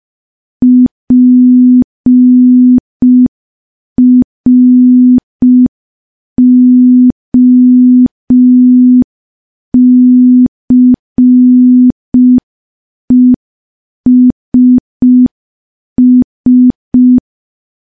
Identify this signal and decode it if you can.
Morse code, heard as a keyed tone, about 5 wpm, 250 Hz. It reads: PROCESS
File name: morse / morse-training-05WPM